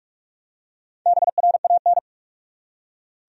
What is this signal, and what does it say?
Morse code, heard as a keyed tone, about 46 wpm, 700 Hz. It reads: BRAN
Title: USA Common Names - 45wpm